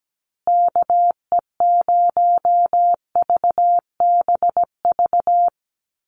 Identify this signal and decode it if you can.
Morse code, heard as a keyed tone, about 17 wpm, 700 Hz. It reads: KE0VBV
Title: USA Call Signs Repeated 3x 17wpm